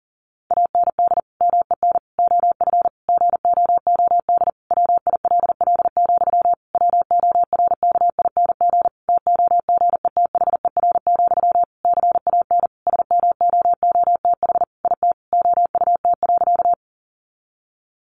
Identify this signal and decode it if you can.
Morse code, heard as a keyed tone, about 37 wpm, 700 Hz. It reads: AND MEN OF GOOD WILL, WORKING TOGETHER, CAN SMOOTH IT OUT.
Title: Sentences from Top 1400 Words 40wpm